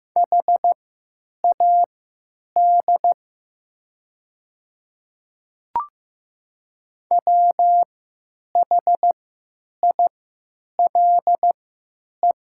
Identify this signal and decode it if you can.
Morse code, heard as a keyed tone, about 15 wpm, spaced as for 9 wpm, 700 Hz. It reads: HAD WHILE
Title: 3x Character Spacing - Top 300 Words - 15wpm